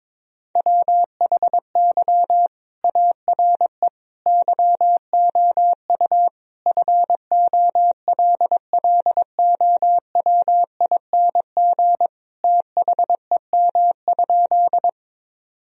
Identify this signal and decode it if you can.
Morse code, heard as a keyed tone, about 22 wpm, 700 Hz. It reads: WHY ARE YOU FOLLOWING THEM?
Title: Sentences from Top 1000 Words - Review Part 4 - 22wpm